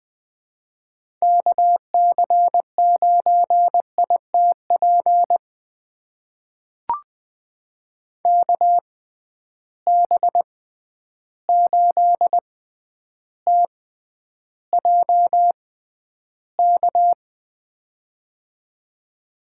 Morse code, heard as a keyed tone, about 21 wpm, spaced as for 10 wpm, 700 Hz. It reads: KC9ITP KB8TJK
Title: USA Call Signs - 6 Characters - 6x Character Spacing - 20wpm